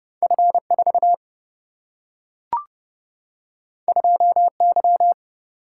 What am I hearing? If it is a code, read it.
Morse code, heard as a keyed tone, about 30 wpm, 700 Hz. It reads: F4 2Y
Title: Single-Letter Single-Number 30wpm